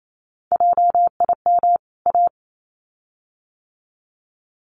Morse code, heard as a keyed tone, about 28 wpm, 700 Hz. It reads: JIM A
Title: CWT Exchange 28wpm